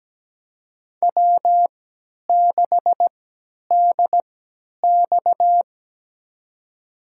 Morse code, heard as a keyed tone, about 17 wpm, spaced as for 10 wpm, 700 Hz. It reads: W6DX
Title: USA Call Signs - 4 Characters - 3x Character Spacing - 17wpm